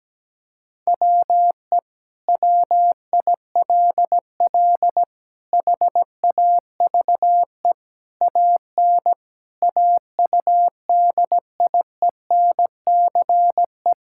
Morse code, heard as a keyed tone, about 17 wpm, 700 Hz. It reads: WE WILL HAVE AN AUDIENCE
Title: Sentences from Top 1000 Words - Review Part 5 - 17wpm